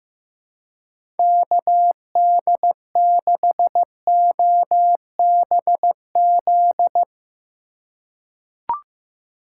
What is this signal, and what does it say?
Morse code, heard as a keyed tone, about 15 wpm, 700 Hz. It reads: KD6OBZ
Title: USA Call Signs - 6 Characters - 3x Character Spacing - 15wpm